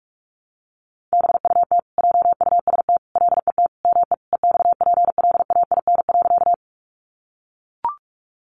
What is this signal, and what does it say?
Morse code, heard as a keyed tone, about 44 wpm, 700 Hz. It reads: BUT JUST LET ME EXPLAIN.
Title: Sentences from Top 1100 Words 45wpm